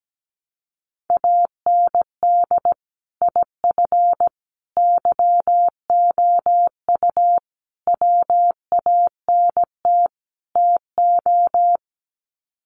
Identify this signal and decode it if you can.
Morse code, heard as a keyed tone, about 17 wpm, 700 Hz. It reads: AND IF YOU WANT TO
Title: Sets of 5 Words Encore 17wpm